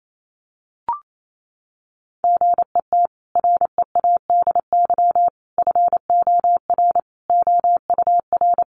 Morse code, heard as a keyed tone, about 28 wpm, 700 Hz. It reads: GET READY FOR OUR
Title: Sentences from Top 1200 Words 28wpm